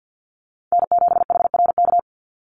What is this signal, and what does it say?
Morse code, heard as a keyed tone, about 50 wpm, 700 Hz. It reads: N7HRK
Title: Call Signs Repeated 2x - 50wpm